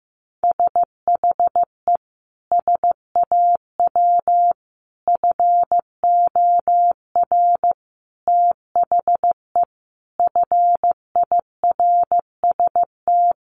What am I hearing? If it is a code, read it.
Morse code, heard as a keyed tone, about 15 wpm, 700 Hz. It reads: SHE SAW FOR THE FIRST
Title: Sentences from Top 1200 Words 15wpm